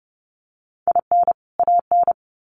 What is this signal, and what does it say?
Morse code, heard as a keyed tone, about 30 wpm, 700 Hz. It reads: IN AN
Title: Sets of 2 Words 30wpm